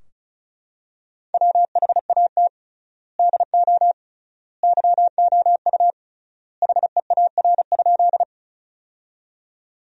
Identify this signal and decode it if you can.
Morse code, heard as a keyed tone, about 36 wpm, 700 Hz. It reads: WHAT DO YOU HEAR?
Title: 3x Word-Spacing - Sentences from Top 300 Words - 35wpm